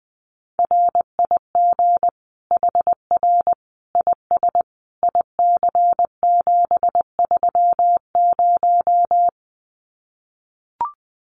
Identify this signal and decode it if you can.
Morse code, heard as a keyed tone, about 20 wpm, 700 Hz. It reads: RIG HR IS IC730
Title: Common QSO Phrases 20wpm